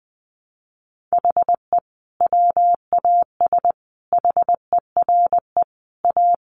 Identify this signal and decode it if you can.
Morse code, heard as a keyed tone, about 20 wpm, 700 Hz. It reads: HE WAS HERE A
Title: Sentences from Top 500 Words 20wpm